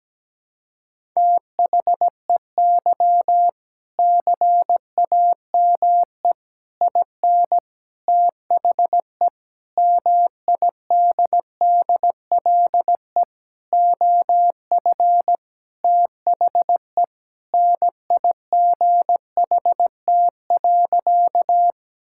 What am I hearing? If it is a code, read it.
Morse code, heard as a keyed tone, about 17 wpm, 700 Hz. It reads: THEY CAME IN THE MIDDLE OF THE NIGHT.
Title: Sentences from Top 1000 Words - Review Part 6 - 17wpm